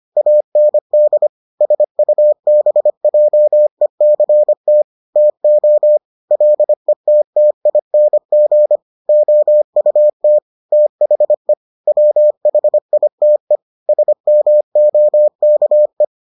Morse code, heard as a keyed tone, about 25 wpm, 600 Hz. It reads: AND SUBJECT TO LETTING OUT THE WHITE SMOKE